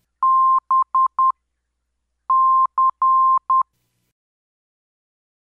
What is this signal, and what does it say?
Morse code, heard as a keyed tone, about 10 wpm, 1050 Hz. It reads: B C